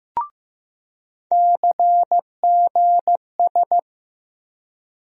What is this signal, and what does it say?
Morse code, heard as a keyed tone, about 15 wpm, 700 Hz. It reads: CGS
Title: USA Call Sign Postfixes - 3 Letters 15wpm